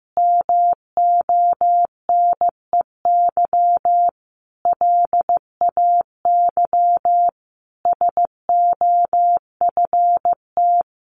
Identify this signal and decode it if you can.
Morse code, heard as a keyed tone, about 15 wpm, 700 Hz. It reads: MONEY LAY SOFT
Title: Sentences from Top 1500 Words 15wpm